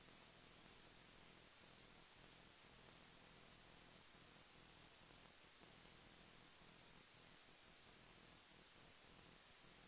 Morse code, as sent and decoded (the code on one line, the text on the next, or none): none